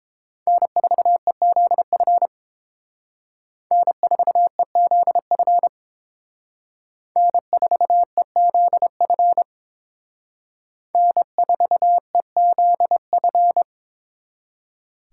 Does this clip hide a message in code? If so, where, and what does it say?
Yes: Morse code from 0.5 to 13.6 s, heard as a keyed tone, about 28 wpm, 700 Hz. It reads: N4EZF N4EZF N4EZF N4EZF